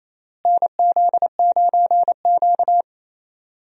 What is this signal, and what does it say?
Morse code, heard as a keyed tone, about 27 wpm, 700 Hz. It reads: NZ9Q
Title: USA Call Signs - 4 Characters - Repeated 3x - 28wpm